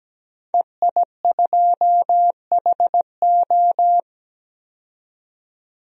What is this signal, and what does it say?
Morse code, heard as a keyed tone, about 17 wpm, 700 Hz. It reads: EI2HO
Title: DXCC-100 Call Signs - 5 Characters - 17wpm